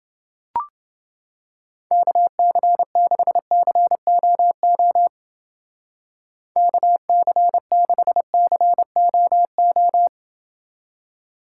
Morse code, heard as a keyed tone, about 28 wpm, 700 Hz. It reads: KC6COO KC6COO